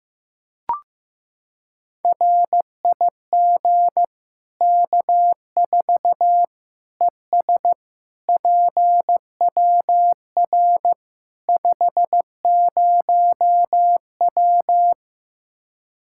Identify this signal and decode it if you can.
Morse code, heard as a keyed tone, about 15 wpm, 700 Hz. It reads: RIG K4 ES PWR 50W
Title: Practice QSO - Exchange 2 - Info - 15wpm